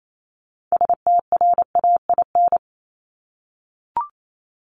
Morse code, heard as a keyed tone, about 28 wpm, 700 Hz. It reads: STRAIN
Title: Top 1751-2000 Words 28wpm